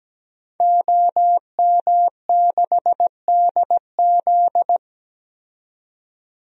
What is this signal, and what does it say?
Morse code, heard as a keyed tone, about 17 wpm, 700 Hz. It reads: OM6DZ
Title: DXCC-100 Call Signs - 5 Characters - 17wpm